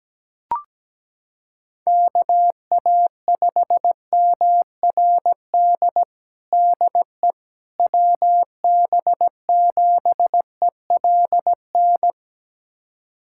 Morse code, heard as a keyed tone, about 17 wpm, 700 Hz. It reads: KA5MRD DE WB7ELN
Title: USA Callsign DE Callsign 17wpm